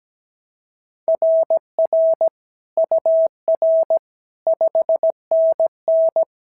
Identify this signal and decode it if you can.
Morse code, heard as a keyed tone, about 16 wpm, 650 Hz. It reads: RR UR 5NN
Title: SOTA QSO Practice 17wpm